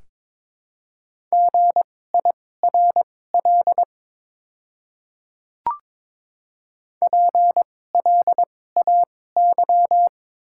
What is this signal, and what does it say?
Morse code, heard as a keyed tone, about 23 wpm, spaced as for 16 wpm, 700 Hz. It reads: GIRL PLAY
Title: 2x Character Spacing - Top 500 Words - 22wpm